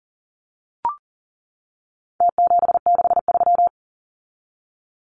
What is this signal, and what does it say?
Morse code, heard as a keyed tone, about 40 wpm, 700 Hz. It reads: T763
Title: Numbers - Contest Style - 4-Digit 40wpm